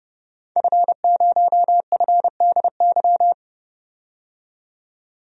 Morse code, heard as a keyed tone, about 30 wpm, 700 Hz. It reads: F0FDY
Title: DXCC-100 Call Signs - 5 Characters - 30wpm